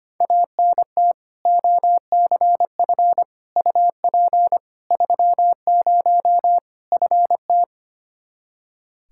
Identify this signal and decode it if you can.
Morse code, heard as a keyed tone, about 25 wpm, 700 Hz. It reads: ANT OCF UP 30 FT
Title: Practice QSO - Exchange 2 - Info - 25wpm